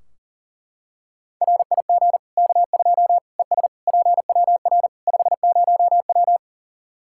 Morse code, heard as a keyed tone, about 37 wpm, 700 Hz. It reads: RIG K2 ES PWR 50W
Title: Practice QSO - Exchange 2 - Info - 40wpm